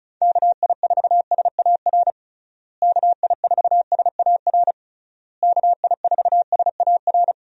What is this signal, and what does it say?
Morse code, heard as a keyed tone, about 36 wpm, 700 Hz. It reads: KI4SAR KI4SAR KI4SAR
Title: USA Call Signs - 6 Characters - Repeated 3x - 35wpm